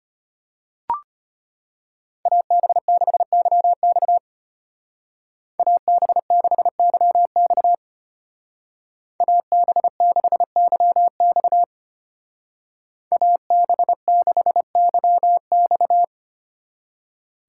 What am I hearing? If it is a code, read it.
Morse code, heard as a keyed tone, about 31 wpm, 700 Hz. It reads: AB6YX AB6YX AB6YX AB6YX
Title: Speed Racing - USA 2x2 Call Signs 25wpm